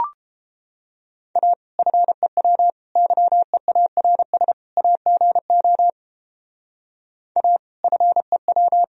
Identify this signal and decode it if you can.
Morse code, heard as a keyed tone, about 31 wpm, 700 Hz. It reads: A FEW YEARS AGO A FEW